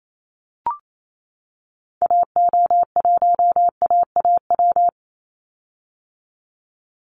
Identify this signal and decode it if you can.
Morse code, heard as a keyed tone, about 28 wpm, 700 Hz. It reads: AO1AAW
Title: Call Signs - 28wpm